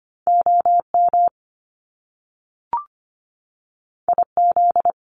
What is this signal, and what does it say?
Morse code, heard as a keyed tone, about 25 wpm, 700 Hz. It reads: OM IZ
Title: Top Two Letter Combinations in Words 25wpm